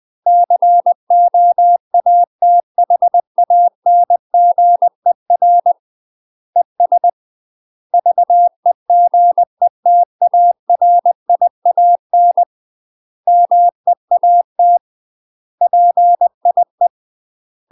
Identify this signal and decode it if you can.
Morse code, heard as a keyed tone, about 20 wpm, 700 Hz. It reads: COATHANGER ES VEGETARIAN MEAT PIE
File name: UK MASTER QSOs TRAINING / 20WPM / UK MASTER QSOs 0026